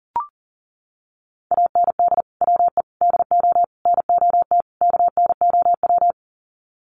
Morse code, heard as a keyed tone, about 40 wpm, 700 Hz. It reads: AND WE DO NOT KNOW
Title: Sets of 5 Words Encore 40wpm